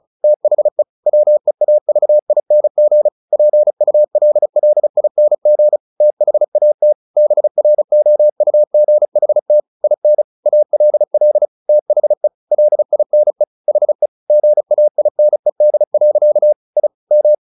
Morse code, heard as a keyed tone, about 34 wpm, 600 Hz. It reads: THE WEAVING PULLING THAT BROUGHT IN ALL THE LINE HE GAINED. I M
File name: the old man and the sea 2 35wpm